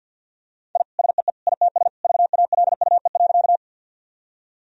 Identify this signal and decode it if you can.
Morse code, heard as a keyed tone, about 50 wpm, 700 Hz. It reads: I SEE ITS VALUE.
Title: Sentences from Top 1000 Words - Review Part 3 - 50wpm